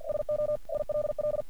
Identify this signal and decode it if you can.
Morse code, heard as a keyed tone, about 47 wpm, 600 Hz. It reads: SO ILL
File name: archived / CRSnet